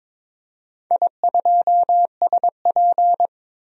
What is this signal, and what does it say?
Morse code, heard as a keyed tone, about 23 wpm, 700 Hz. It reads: I2SP
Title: DXCC-100 Call Signs - 4 Characters - 22wpm